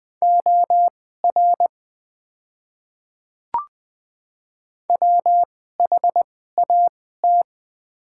Morse code, heard as a keyed tone, about 20 wpm, spaced as for 14 wpm, 700 Hz. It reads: OR WHAT
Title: 2x Character Spacing - Top 300 Words - 20wpm